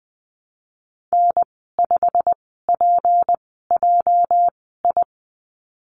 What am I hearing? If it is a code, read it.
Morse code, heard as a keyed tone, about 20 wpm, spaced as for 14 wpm, 700 Hz. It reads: N5PJI
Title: 219 Lesson 35g - Number 0 - Call Signs 20wpm